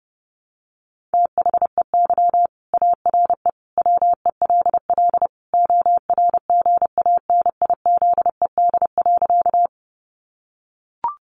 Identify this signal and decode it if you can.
Morse code, heard as a keyed tone, about 30 wpm, 700 Hz. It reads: THEY ARE WELL ORGANIZED.
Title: Sentences from Top 1300 Words 30wpm